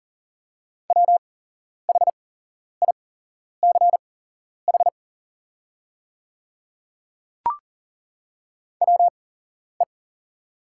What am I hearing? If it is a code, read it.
Morse code, heard as a keyed tone, about 40 wpm, spaced as for 11 wpm, 690 Hz. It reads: WHICH WE
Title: 8x Character Spacing - Top 100 Words - 40wpm